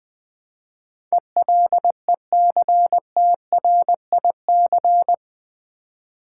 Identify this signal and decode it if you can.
Morse code, heard as a keyed tone, about 20 wpm, 700 Hz. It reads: ELECTRIC
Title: Build A Word - 2 Letters At A Time - 8-Letter Words - 20wpm